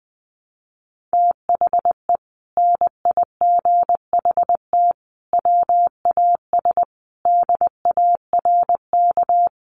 Morse code, heard as a keyed tone, about 20 wpm, 700 Hz. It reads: THE NIGHT WAS DARK
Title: Sentences from Top 1300 Words 20wpm